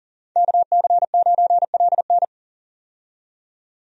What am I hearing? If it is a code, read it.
Morse code, heard as a keyed tone, about 40 wpm, 700 Hz. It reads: KC9LN
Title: USA Call Signs - 5 Characters - 40wpm